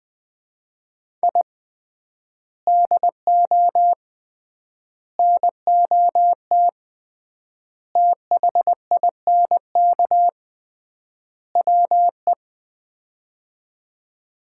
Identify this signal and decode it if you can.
Morse code, heard as a keyed tone, about 20 wpm, 700 Hz. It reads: I DO NOT THINK WE
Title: 3x Word-Spacing - Sets of 5 Words - 20wpm